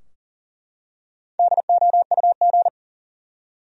Morse code, heard as a keyed tone, about 40 wpm, 700 Hz. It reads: DOUG
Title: USA Common Names - Max 5 Letters - 40wpm